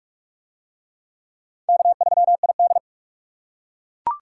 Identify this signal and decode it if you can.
Morse code, heard as a keyed tone, about 41 wpm, 700 Hz. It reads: K3ID